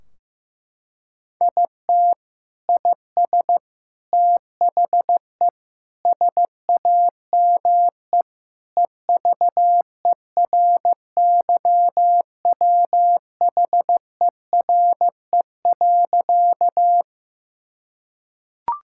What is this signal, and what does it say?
Morse code, heard as a keyed tone, about 15 wpm, 700 Hz. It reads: IT IS THE SAME EVERYWHERE.
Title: Sentences from Top 1600 Words 15wpm